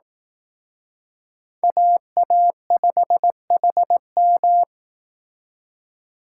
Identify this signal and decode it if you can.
Morse code, heard as a keyed tone, about 18 wpm, 700 Hz. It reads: AA5HM
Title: Speed Racing - USA 2x2 Call Signs 15wpm